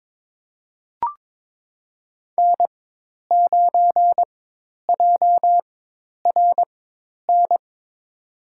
Morse code, heard as a keyed tone, about 22 wpm, spaced as for 10 wpm, 700 Hz. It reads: N9JRN
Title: USA Call Signs - 5 Characters - 4x Character Spacing - 22wpm